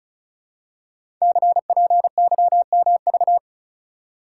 Morse code, heard as a keyed tone, about 35 wpm, 700 Hz. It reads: CPYMV
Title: Code Groups - 5 Letter:Numbers - 35wpm